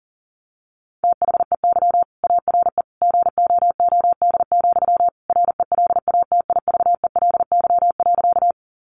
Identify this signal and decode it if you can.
Morse code, heard as a keyed tone, about 40 wpm, 700 Hz. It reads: THEY ARE GOOD, RELATIVELY.
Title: Sentences from Top 1100 Words 40wpm